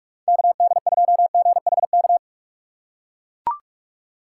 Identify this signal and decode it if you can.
Morse code, heard as a keyed tone, about 44 wpm, 700 Hz. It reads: KD2GHK
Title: USA Call Signs - 6 Characters - 3x Character Spacing - 45wpm